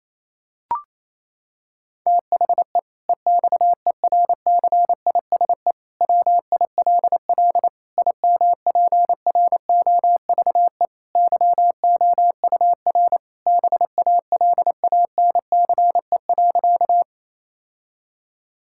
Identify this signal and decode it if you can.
Morse code, heard as a keyed tone, about 27 wpm, 700 Hz. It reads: THE EXERCISE WILL IMPROVE YOUR BALANCE.
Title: Sentences from Top 1000 Words - Review Part 6 - 28wpm